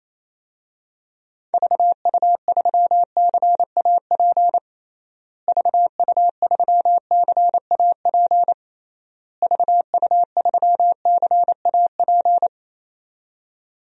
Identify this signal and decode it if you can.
Morse code, heard as a keyed tone, about 28 wpm, 700 Hz. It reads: VU3CAP VU3CAP VU3CAP
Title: DXCC-100 Call Signs - 6 Characters - Repeated 3x - 28wpm